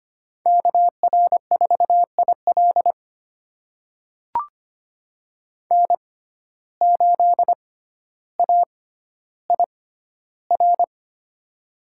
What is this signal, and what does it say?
Morse code, heard as a keyed tone, about 25 wpm, spaced as for 13 wpm, 700 Hz. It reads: KR4IL N8AIR
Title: USA Call Signs - 5 Characters - 6x Character Spacing - 25wpm